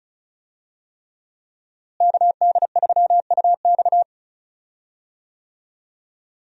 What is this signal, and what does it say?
Morse code, heard as a keyed tone, about 36 wpm, 700 Hz. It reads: KD3UX
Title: USA Call Signs - 5 Characters - Repeated 3x - 35wpm